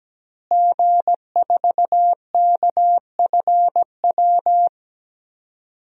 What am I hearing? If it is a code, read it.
Morse code, heard as a keyed tone, about 17 wpm, 700 Hz. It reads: G4KFW
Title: DXCC-100 Call Signs - 5 Characters - Repeated 2x - 17wpm